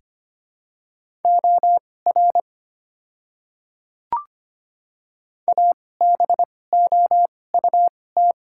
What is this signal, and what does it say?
Morse code, heard as a keyed tone, about 25 wpm, spaced as for 18 wpm, 700 Hz. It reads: OR ABOUT
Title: 2x Character Spacing - Top 100 Words - 25wpm